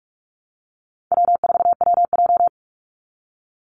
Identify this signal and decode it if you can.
Morse code, heard as a keyed tone, about 44 wpm, 700 Hz. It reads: W4WJ
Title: Easy Call Signs Repeated 3x - 45wpm